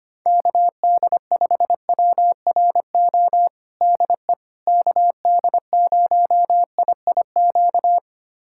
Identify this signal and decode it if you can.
Morse code, heard as a keyed tone, about 25 wpm, 700 Hz. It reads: KD5WRO DE KD0IIQ
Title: USA Callsign DE Callsign - 6 Characters - 25wpm